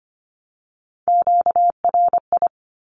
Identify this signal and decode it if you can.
Morse code, heard as a keyed tone, about 25 wpm, 700 Hz. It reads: QRI
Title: Q-Codes 25wpm